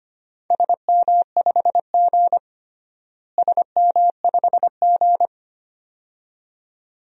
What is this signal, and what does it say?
Morse code, heard as a keyed tone, about 25 wpm, 700 Hz. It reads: SM5G SM5G
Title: DXCC-100 Call Signs - 4 Characters - Repeated 2x - 25wpm